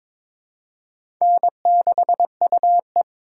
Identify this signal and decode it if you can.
Morse code, heard as a keyed tone, about 21 wpm, 700 Hz. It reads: N6UE
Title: USA Call Signs - 4 Characters - 2x Character Spacing - 22wpm